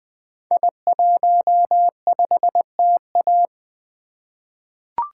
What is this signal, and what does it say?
Morse code, heard as a keyed tone, about 20 wpm, 700 Hz. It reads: I15TA
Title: Code Groups - 5 Letter:Numbers - 20wpm